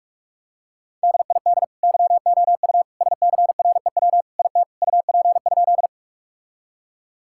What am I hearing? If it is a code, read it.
Morse code, heard as a keyed tone, about 46 wpm, 700 Hz. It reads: DID YOU SCREW IT UP?
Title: Sentences from Top 1600 Words 45wpm